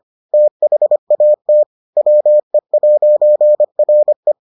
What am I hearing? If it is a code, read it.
Morse code, heard as a keyed tone, about 25 wpm, 600 Hz. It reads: THAT WE'RE